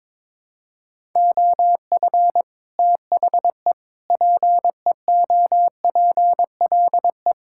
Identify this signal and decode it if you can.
Morse code, heard as a keyed tone, about 22 wpm, 700 Hz. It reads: OF THE PEOPLE
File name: Sets of 9 Words - 22wpm